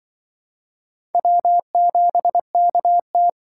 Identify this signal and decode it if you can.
Morse code, heard as a keyed tone, about 25 wpm, 700 Hz. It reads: W7KT
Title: Speed Racing - USA 1x2 Call Signs 20wpm